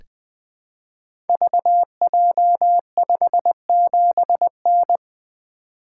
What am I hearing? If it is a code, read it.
Morse code, heard as a keyed tone, about 20 wpm, 700 Hz. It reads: VJ57N
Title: Code Groups - 5 Letter:Numbers - 20wpm